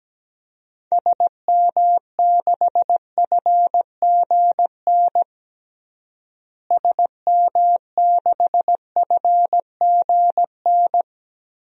Morse code, heard as a keyed tone, about 17 wpm, 700 Hz. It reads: SM6FGN SM6FGN
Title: DXCC-100 Call Signs - 6 Characters - Repeated 3x - 17wpm